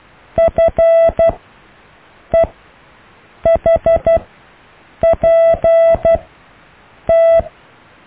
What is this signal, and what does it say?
Morse code, heard as a keyed tone, about 12 wpm, spaced as for 6 wpm, 660 Hz. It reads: FEHPT
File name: cw / EKTPV+BFHNW